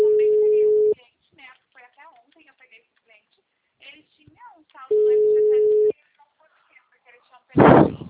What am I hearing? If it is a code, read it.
Ringback tone (Europe, 425 Hz), with speech.